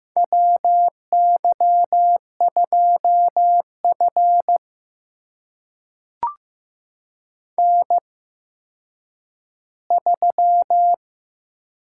Morse code, heard as a keyed tone, about 15 wpm, spaced as for 9 wpm, 700 Hz. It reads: WY2F N3
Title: USA Call Signs - 4 Characters - 8x Character Spacing - 15wpm